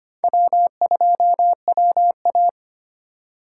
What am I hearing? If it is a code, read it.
Morse code, heard as a keyed tone, about 25 wpm, 700 Hz. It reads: W2WA
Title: USA Call Signs - 4 Characters - 4x Character Spacing - 25wpm